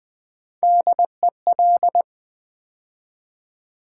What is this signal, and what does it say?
Morse code, heard as a keyed tone, about 20 wpm, 700 Hz. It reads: DEL
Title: Build A Word - 3 Letters At A Time - 6-Letter Words - 20wpm